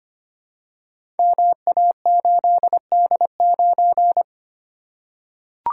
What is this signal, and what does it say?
Morse code, heard as a keyed tone, about 25 wpm, 700 Hz. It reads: MA8D9
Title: Code Groups - 5 Letter:Numbers - 25wpm